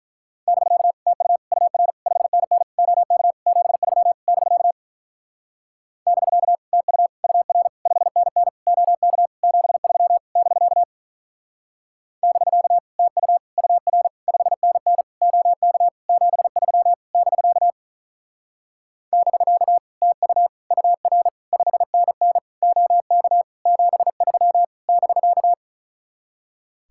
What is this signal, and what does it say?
Morse code, heard as a keyed tone, about 43 wpm, 700 Hz. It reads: <BK> TU UR 5NN OK 73 <BK> <BK> TU UR 5NN OK 73 <BK> <BK> TU UR 5NN OK 73 <BK> <BK> TU UR 5NN OK 73 <BK>
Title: Speed-Racing Common QSO Phrases - 35wpm